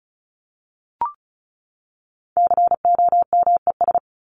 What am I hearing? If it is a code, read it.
Morse code, heard as a keyed tone, about 34 wpm, 700 Hz. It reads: COMES
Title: ES:S Suffix - Top 100 Words with Base Words 35wpm